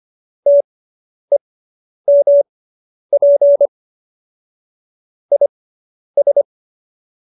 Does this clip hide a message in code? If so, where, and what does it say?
Yes: Morse code from 0.5 to 6.4 s, heard as a keyed tone, about 25 wpm, spaced as for 10 wpm, 570 Hz. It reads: TEMP IS